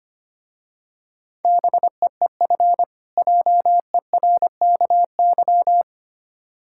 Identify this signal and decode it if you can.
Morse code, heard as a keyed tone, about 25 wpm, 700 Hz. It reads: BEEF JERKY
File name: practice / 25wpm / Difficult Words with Hints - 25wpm